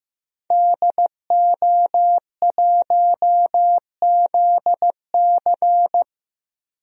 Morse code, heard as a keyed tone, about 15 wpm, 700 Hz. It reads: DO1ZC
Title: DXCC-100 Call Signs - 5 Characters - Repeated 3x - 15wpm